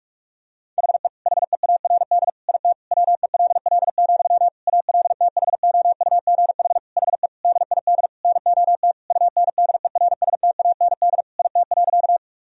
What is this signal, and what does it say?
Morse code, heard as a keyed tone, about 41 wpm, 700 Hz. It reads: HE HEARD IT WELL, ALTHOUGH HE DID NOT UNDERSTAND IT.
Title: Sentences from Top 1000 Words - Review Part 3 - 45wpm